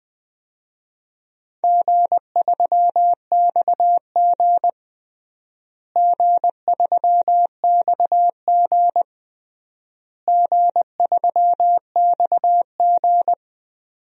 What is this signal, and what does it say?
Morse code, heard as a keyed tone, about 20 wpm, 700 Hz. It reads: G3XG G3XG G3XG
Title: DXCC-100 Call Signs - 4 Characters - Repeated 3x - 20wpm